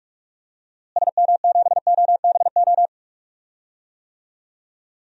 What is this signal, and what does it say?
Morse code, heard as a keyed tone, about 46 wpm, 700 Hz. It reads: SM7OBO
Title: DXCC-100 Call Signs - 6 Characters - Repeated 3x - 45wpm